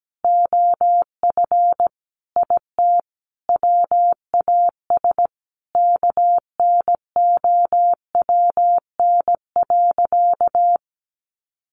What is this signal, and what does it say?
Morse code, heard as a keyed tone, about 17 wpm, 700 Hz. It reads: OF IT WAS KNOWN.
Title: Sentences from Top 1400 Words 17wpm